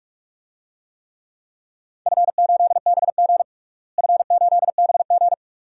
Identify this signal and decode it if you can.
Morse code, heard as a keyed tone, about 42 wpm, 700 Hz. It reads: F8BG F8BG
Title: DXCC-100 Call Signs - 4 Characters - Repeated 2x - 45wpm